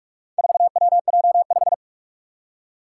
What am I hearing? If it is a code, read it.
Morse code, heard as a keyed tone, about 46 wpm, 700 Hz. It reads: 4WJ5
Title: Four Challenging Letter-Number Combinations 45wpm